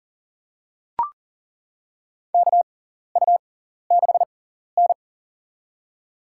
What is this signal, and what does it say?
Morse code, heard as a keyed tone, about 40 wpm, spaced as for 14 wpm, 700 Hz. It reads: KU6N
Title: USA Call Signs - 4 Characters - 6x Character Spacing - 40wpm